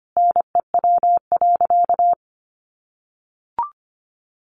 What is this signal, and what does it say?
Morse code, heard as a keyed tone, about 25 wpm, 700 Hz. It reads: NEW.
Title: Sentences from Top 1500 Words 25wpm